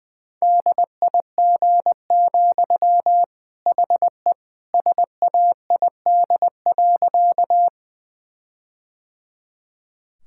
Morse code, heard as a keyed tone, about 20 wpm, 700 Hz. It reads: DIG, HE SAID.
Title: Sentences from Top 1600 Words 20wpm